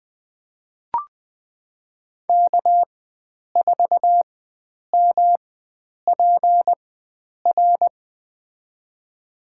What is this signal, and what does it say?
Morse code, heard as a keyed tone, about 20 wpm, spaced as for 9 wpm, 700 Hz. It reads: K4MPR
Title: USA Call Signs - 5 Characters - 4x Character Spacing - 20wpm